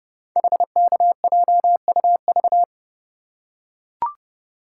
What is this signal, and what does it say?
Morse code, heard as a keyed tone, about 30 wpm, 700 Hz. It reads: HKJUV
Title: Code Groups - 5 Letters - 30wpm